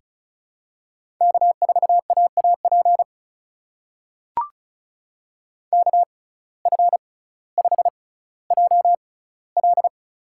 Morse code, heard as a keyed tone, about 34 wpm, spaced as for 18 wpm, 700 Hz. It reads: K4AAP KF5JL
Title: USA Call Signs - 5 Characters - 6x Character Spacing - 35wpm